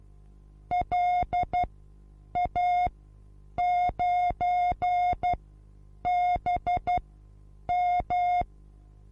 Morse code, heard as a keyed tone, about 12 wpm, spaced as for 8 wpm, 720 Hz. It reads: LA9BM
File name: Onlinelehrgang / morsen / DJ4UF Morsen23